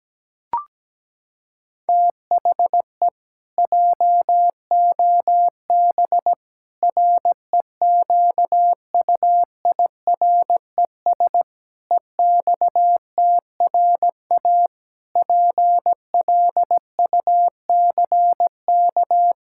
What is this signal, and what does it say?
Morse code, heard as a keyed tone, about 17 wpm, 700 Hz. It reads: THE JOB REQUIRES EXTRA PLUCK
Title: Pangram Challenge v2 - 17wpm